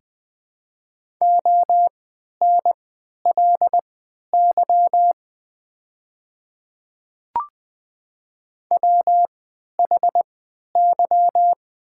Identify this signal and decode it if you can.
Morse code, heard as a keyed tone, about 20 wpm, spaced as for 11 wpm, 700 Hz. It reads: ONLY WHY
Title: 3x Character Spacing - Top 200 Words - 20wpm